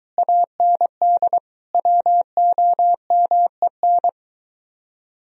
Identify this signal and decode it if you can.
Morse code, heard as a keyed tone, about 22 wpm, 700 Hz. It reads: AND WOMEN